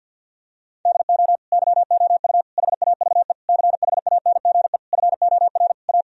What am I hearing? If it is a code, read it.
Morse code, heard as a keyed tone, about 50 wpm, 700 Hz. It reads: DO YOU HAVE CHANGE FOR A
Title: Sentences from Top 1000 Words - Review Part 2 - 50wpm